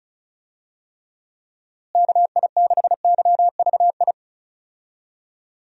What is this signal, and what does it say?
Morse code, heard as a keyed tone, about 36 wpm, 700 Hz. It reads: KI6YVI
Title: USA Call Signs - 6 Characters - Repeated 2x - 35wpm